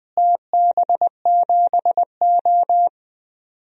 Morse code, heard as a keyed tone, about 20 wpm, 700 Hz. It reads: TB7O
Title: DXCC-100 Call Signs - 4 Characters - Repeated 2x - 20wpm